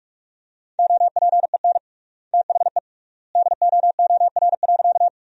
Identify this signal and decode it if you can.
Morse code, heard as a keyed tone, about 46 wpm, 700 Hz. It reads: OPEN THE DOOR.
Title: 3x Word-Spacing - Sentences from Top 300 Words - 45wpm